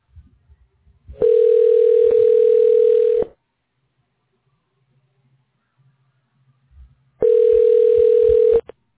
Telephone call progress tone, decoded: ringback tone